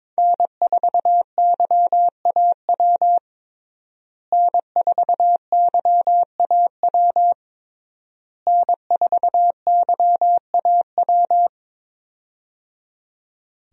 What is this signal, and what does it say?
Morse code, heard as a keyed tone, about 22 wpm, 700 Hz. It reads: N4YAW N4YAW N4YAW
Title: USA Call Signs - 5 Characters - Repeated 3x - 22wpm